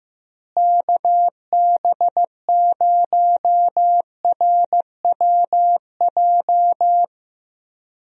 Morse code, heard as a keyed tone, about 15 wpm, 700 Hz. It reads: KB0RWJ
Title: USA Call Signs - 6 Characters - Repeated 2x - 15wpm